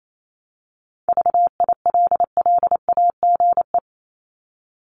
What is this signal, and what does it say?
Morse code, heard as a keyed tone, about 28 wpm, 700 Hz. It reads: VILLAGE
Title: Top 1751-2000 Words 28wpm